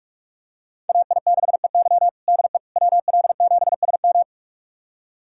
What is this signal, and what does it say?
Morse code, heard as a keyed tone, about 46 wpm, 700 Hz. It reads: AI6EY DE WL7SM
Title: USA Callsign DE Callsign - 5 Characters - 45wpm